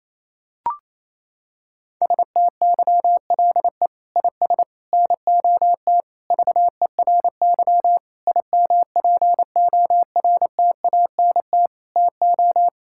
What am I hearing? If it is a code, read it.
Morse code, heard as a keyed tone, about 28 wpm, 700 Hz. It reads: STYLE IS NOT VERY IMPORTANT TO
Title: Sentences from Top 1000 Words - Review Part 5 - 28wpm